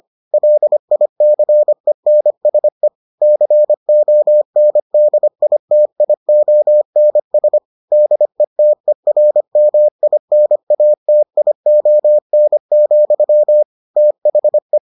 Morse code, heard as a keyed tone, about 25 wpm, 600 Hz. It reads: LICENSE CONDITIONS DETERMINATION, THE